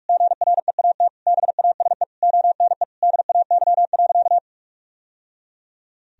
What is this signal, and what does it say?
Morse code, heard as a keyed tone, about 41 wpm, 700 Hz. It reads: GREAT BASE ONE DAY.
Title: Sentences from Top 1000 Words - Review Part 1 - 45wpm